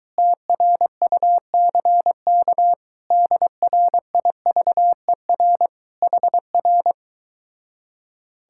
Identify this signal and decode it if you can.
Morse code, heard as a keyed tone, about 23 wpm, 700 Hz. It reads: TRUCK DRIVER HR